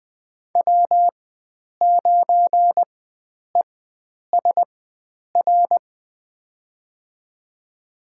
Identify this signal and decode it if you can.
Morse code, heard as a keyed tone, about 20 wpm, spaced as for 9 wpm, 700 Hz. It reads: W9ESR
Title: USA Call Signs - 5 Characters - 4x Character Spacing - 20wpm